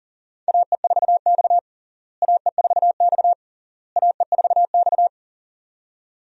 Morse code, heard as a keyed tone, about 40 wpm, 700 Hz. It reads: AE4X AE4X AE4X
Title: USA Call Signs - 4 Characters - Repeated 3x - 40wpm